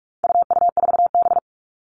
Morse code, heard as a keyed tone, about 44 wpm, 700 Hz. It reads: UU4B
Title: All Cut Numbers - 4-Digit 45wpm